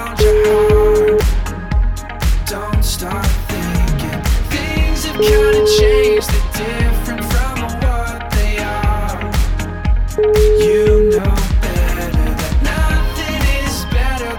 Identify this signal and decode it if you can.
Ringback tone (Europe, 425 Hz), with noise.